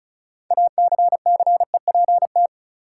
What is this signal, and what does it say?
Morse code, heard as a keyed tone, about 34 wpm, 700 Hz. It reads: ACCEPT
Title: Sentences from Top 1300 Words - New Word Drill Alt - 35wpm